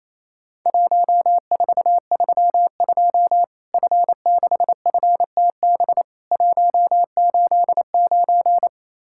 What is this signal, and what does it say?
Morse code, heard as a keyed tone, about 28 wpm, 700 Hz. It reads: 1432 F6FTB 189
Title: Worked All Europe DX Contest - QTC Exchange - 28wpm